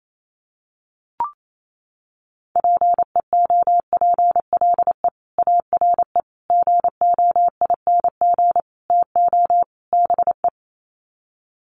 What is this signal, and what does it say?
Morse code, heard as a keyed tone, about 28 wpm, 700 Hz. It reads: PEOPLE ARE GOING TO BE
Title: Sets of 5 Words Encore 28wpm